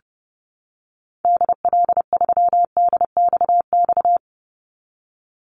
Morse code, heard as a keyed tone, about 30 wpm, 700 Hz. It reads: DL3DXX
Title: Call Signs Repeated 3x - 30wpm